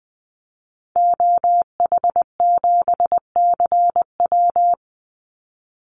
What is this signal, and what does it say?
Morse code, heard as a keyed tone, about 20 wpm, 700 Hz. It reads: OH7CW
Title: Call Signs Repeated 3x - 20wpm